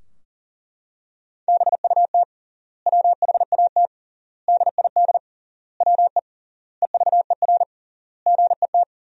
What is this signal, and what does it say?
Morse code, heard as a keyed tone, about 40 wpm, 700 Hz. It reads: BUT WHAT DID WE EVER GET
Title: 3x Word-Spacing - Sentences from Top 400 Words - 40wpm